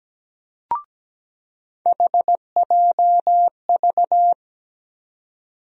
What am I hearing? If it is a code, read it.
Morse code, heard as a keyed tone, about 17 wpm, 700 Hz. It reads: HJV
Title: USA Call Sign Postfixes - 3 Letters 17wpm